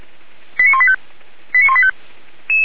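Five-tone selective call (ZVEI), heard as quick runs of five tones, tones of about 80 ms. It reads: 89185
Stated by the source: ZVEI five-tone selective call received by SDR, recorded by the Feuerwehr Orlamünde fire brigade.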